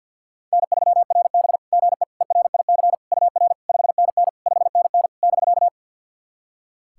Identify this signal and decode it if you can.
Morse code, heard as a keyed tone, about 50 wpm, 700 Hz. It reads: N3RB GE ERIC UR 5NN 5NN <BK>
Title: SOTA QSO Practice 50wpm